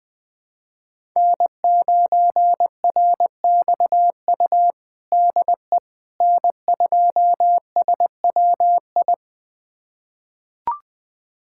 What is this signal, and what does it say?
Morse code, heard as a keyed tone, about 20 wpm, 700 Hz. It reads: N9RXU DE N2SWI
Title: USA Callsign DE Callsign - 5 Characters - 20wpm